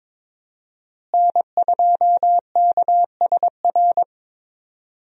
Morse code, heard as a keyed tone, about 21 wpm, 700 Hz. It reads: N2KSR
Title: USA Call Signs - 5 Characters - Repeated 3x - 22wpm